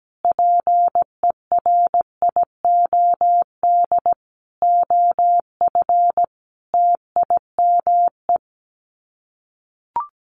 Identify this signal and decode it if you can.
Morse code, heard as a keyed tone, about 17 wpm, 700 Hz. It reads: PERIOD OF TIME
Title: Sets of 5 Words Encore2 17wpm